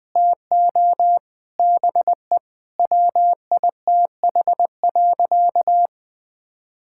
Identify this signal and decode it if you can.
Morse code, heard as a keyed tone, about 20 wpm, 700 Hz. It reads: TO BE WITH.
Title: Sentences from Top 1000 Words - Review Part 5 - 20wpm